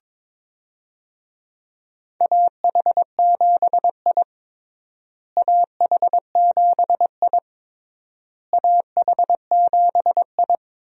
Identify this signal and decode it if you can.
Morse code, heard as a keyed tone, about 22 wpm, 700 Hz. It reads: AH7I AH7I AH7I
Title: USA Call Signs - 4 Characters - Repeated 3x - 22wpm